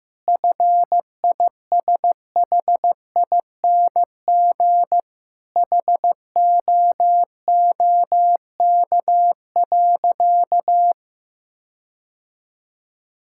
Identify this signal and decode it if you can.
Morse code, heard as a keyed tone, about 15 wpm, 700 Hz. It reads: FISHING HOOK.
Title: Sentences from Top 1600 Words 15wpm